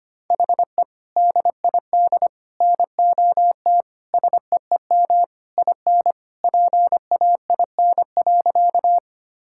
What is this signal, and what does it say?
Morse code, heard as a keyed tone, about 25 wpm, 700 Hz. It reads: HE DID NOT SEEM IN PAIN.
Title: Sentences from Top 1600 Words 25wpm